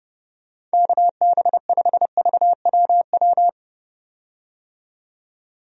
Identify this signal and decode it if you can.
Morse code, heard as a keyed tone, about 30 wpm, 700 Hz. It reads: KB5VWW
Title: USA Call Signs - 6 Characters - 30wpm